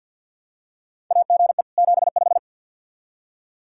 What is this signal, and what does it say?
Morse code, heard as a keyed tone, about 47 wpm, 700 Hz. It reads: AGE 75
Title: K1USN SST QSO Practice 50wpm